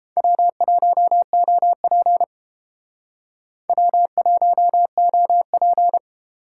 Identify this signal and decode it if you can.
Morse code, heard as a keyed tone, about 29 wpm, 700 Hz. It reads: W1OP W1OP